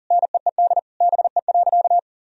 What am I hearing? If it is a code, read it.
Morse code, heard as a keyed tone, about 37 wpm, 700 Hz. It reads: NEED BE.
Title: Sentences from Top 1000 Words - Review Part 1 - 40wpm